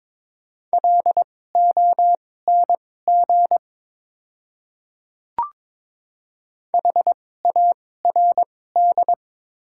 Morse code, heard as a keyed tone, about 22 wpm, spaced as for 16 wpm, 700 Hz. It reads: LONG HARD
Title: 2x Character Spacing - Top 300 Words - 22wpm